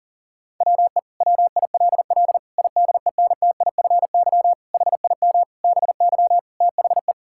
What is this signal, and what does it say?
Morse code, heard as a keyed tone, about 37 wpm, 700 Hz. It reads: WE WILL IDENTIFY HIM BY THE
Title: Sentences from Top 1000 Words - Review Part 6 - 40wpm